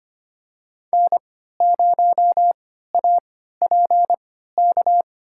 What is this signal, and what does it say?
Morse code, heard as a keyed tone, about 25 wpm, spaced as for 14 wpm, 700 Hz. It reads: N0APK
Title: USA Call Signs - 5 Characters - 3x Character Spacing - 25wpm